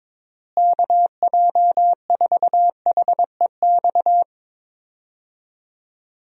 Morse code, heard as a keyed tone, about 21 wpm, 700 Hz. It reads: KJ4HEX
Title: USA Call Signs - 6 Characters - Repeated 2x - 22wpm